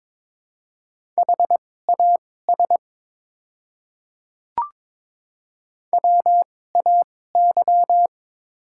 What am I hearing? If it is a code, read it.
Morse code, heard as a keyed tone, about 22 wpm, spaced as for 16 wpm, 690 Hz. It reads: HAS WAY
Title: 2x Character Spacing - Top 200 Words - 22wpm